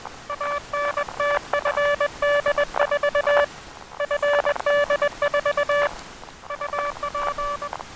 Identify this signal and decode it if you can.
Morse code, heard as a keyed tone, about 20 wpm, 580 Hz, with noise. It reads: ANT FD4 FD4 UP
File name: attachments / 23.04.2017 DSB TRX OM3JA